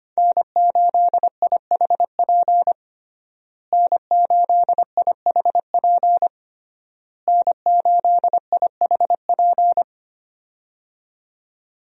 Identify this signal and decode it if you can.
Morse code, heard as a keyed tone, about 25 wpm, 700 Hz. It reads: N8IHP N8IHP N8IHP
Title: USA Call Signs - 5 Characters - Repeated 3x - 25wpm